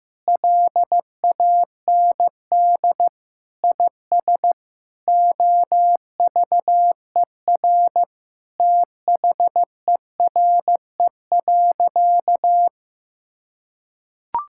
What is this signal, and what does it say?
Morse code, heard as a keyed tone, about 15 wpm, 700 Hz. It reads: LAND IS OVER THERE.
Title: Sentences from Top 1600 Words 15wpm